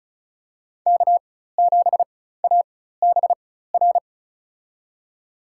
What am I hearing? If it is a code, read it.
Morse code, heard as a keyed tone, about 36 wpm, spaced as for 16 wpm, 700 Hz. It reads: K7ABR
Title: USA Call Signs - 5 Characters - 4x Character Spacing - 35wpm